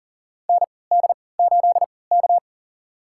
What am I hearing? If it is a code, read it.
Morse code, heard as a keyed tone, about 40 wpm, spaced as for 23 wpm, 700 Hz. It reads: ND8K
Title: USA Call Signs - 4 Characters - 3x Character Spacing - 40wpm